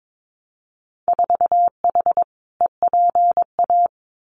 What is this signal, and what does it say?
Morse code, heard as a keyed tone, about 22 wpm, 700 Hz. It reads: 4H EPA
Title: Winter Field Day Exchanges 22wpm